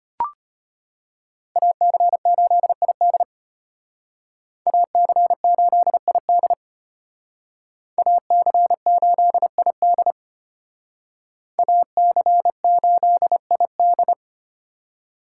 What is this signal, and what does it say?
Morse code, heard as a keyed tone, about 32 wpm, 700 Hz. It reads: AC8ID AC8ID AC8ID AC8ID